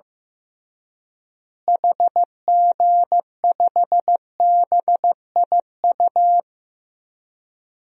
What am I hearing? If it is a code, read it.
Morse code, heard as a keyed tone, about 15 wpm, 700 Hz. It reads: HG5BIU
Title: DXCC-100 Call Signs - 6 Characters - Repeated 3x - 15wpm